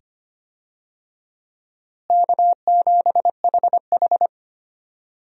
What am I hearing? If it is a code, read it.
Morse code, heard as a keyed tone, about 25 wpm, 700 Hz. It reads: K7HH
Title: USA Call Signs - 4 Characters - Repeated 2x - 25wpm